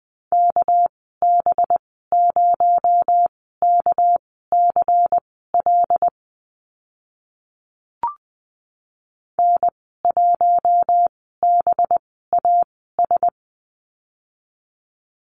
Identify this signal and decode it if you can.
Morse code, heard as a keyed tone, about 20 wpm, spaced as for 14 wpm, 700 Hz. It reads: KB0KCL N1BAS
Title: 219 Lesson 35g - Number 0 - Call Signs 20wpm